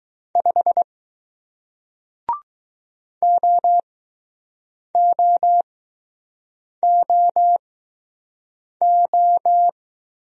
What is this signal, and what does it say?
Morse code, heard as a keyed tone, about 20 wpm, spaced as for 6 wpm, 700 Hz. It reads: 5 OOOO